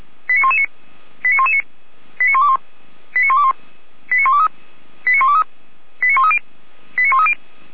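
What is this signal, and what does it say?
Five-tone selective call (ZVEI), heard as quick runs of five tones, tones of about 70 ms. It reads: 89119 89119 89121 89121 89123 89123 89130 89130
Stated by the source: Five-tone ZVEI selective call 89010 received by SDR.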